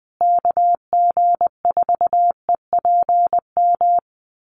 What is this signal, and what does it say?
Morse code, heard as a keyed tone, about 20 wpm, 700 Hz. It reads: KG4EPM
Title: USA Call Signs Repeated 2x 20wpm